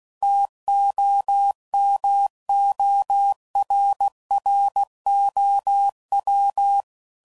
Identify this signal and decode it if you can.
Morse code, heard as a keyed tone, about 16 wpm, 800 Hz. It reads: TOMORROW